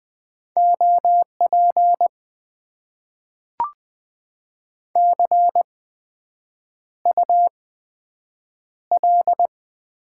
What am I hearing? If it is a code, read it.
Morse code, heard as a keyed tone, about 20 wpm, spaced as for 7 wpm, 700 Hz. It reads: OP CUL
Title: Top 90 QSO Elements - 8x Character Spacing - 20wpm